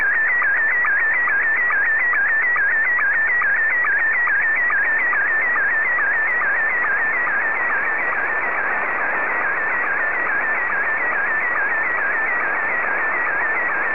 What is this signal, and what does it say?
SSTV (slow-scan television), Scottie 1.